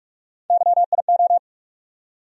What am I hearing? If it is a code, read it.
Morse code, heard as a keyed tone, about 46 wpm, 700 Hz. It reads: YIO
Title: USA Call Sign Postfixes - 3 Letters 45wpm